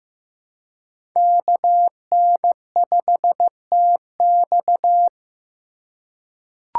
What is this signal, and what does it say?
Morse code, heard as a keyed tone, about 15 wpm, 700 Hz. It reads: KN5TX
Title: K1USN SST QSO Practice - 15wpm